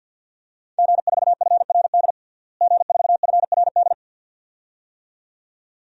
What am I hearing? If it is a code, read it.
Morse code, heard as a keyed tone, about 53 wpm, 700 Hz. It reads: G4FRD G4FRD
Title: DXCC-100 Call Signs - 5 Characters - Repeated 2x - 50wpm